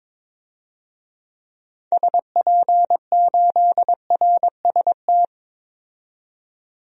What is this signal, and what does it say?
Morse code, heard as a keyed tone, about 22 wpm, 700 Hz. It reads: SP8RST
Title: DXCC-100 Call Signs - 6 Characters - 22wpm